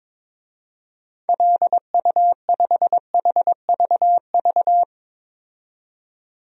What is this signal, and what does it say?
Morse code, heard as a keyed tone, about 22 wpm, 700 Hz. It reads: LU5HVV
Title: DXCC-100 Call Signs - 6 Characters - 22wpm